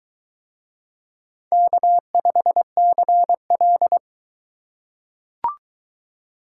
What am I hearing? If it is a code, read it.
Morse code, heard as a keyed tone, about 22 wpm, 700 Hz. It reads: K5CL